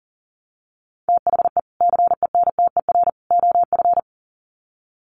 Morse code, heard as a keyed tone, about 40 wpm, 700 Hz. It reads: THE CENTER OF
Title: Sets of 3 Words 40wpm